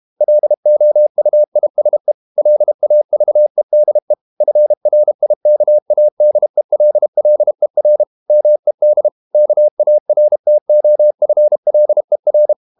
Morse code, heard as a keyed tone, about 32 wpm, 600 Hz. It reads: LOUISE LAVEDE FRIKADELLER MED KARTOFLER
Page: Ferie 32 wpm